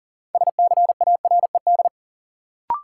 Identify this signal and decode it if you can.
Morse code, heard as a keyed tone, about 40 wpm, 700 Hz. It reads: SCARED
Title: Sentences from Top 700 Words - New Word Drill - 40wpm